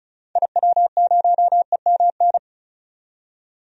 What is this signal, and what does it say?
Morse code, heard as a keyed tone, about 36 wpm, 700 Hz. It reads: IW0EMN
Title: DXCC-100 Call Signs - 6 Characters - 35wpm